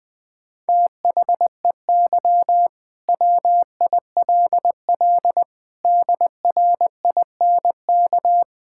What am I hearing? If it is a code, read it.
Morse code, heard as a keyed tone, about 20 wpm, 700 Hz. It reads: THEY WILL DRINK
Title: Sentences from Top 1600 Words 20wpm